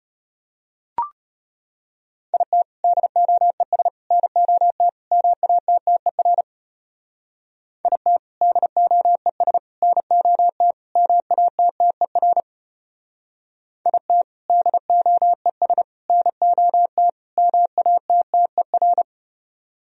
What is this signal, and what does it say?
Morse code, heard as a keyed tone, about 33 wpm, 700 Hz. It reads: IT DOES NOT MATTER IT DOES NOT MATTER IT DOES NOT MATTER